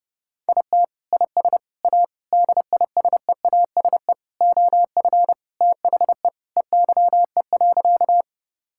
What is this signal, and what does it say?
Morse code, heard as a keyed tone, about 32 wpm, 700 Hz. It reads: IT IS A DISEASE OF THE EYE.
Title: Sentences from Top 1000 Words - Review Part 6 - 30wpm